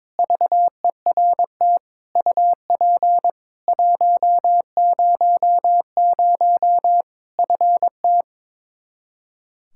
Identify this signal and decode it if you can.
Morse code, heard as a keyed tone, about 22 wpm, 700 Hz. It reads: VERT UP 100 FT
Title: Practice QSO - Exchange 2 - Info - 22wpm